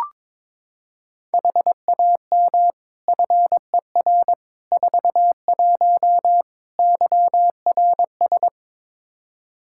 Morse code, heard as a keyed tone, about 22 wpm, 700 Hz. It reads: HAM FER 41 YRS
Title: Practice QSO - Exchange 3 - More Info - 22wpm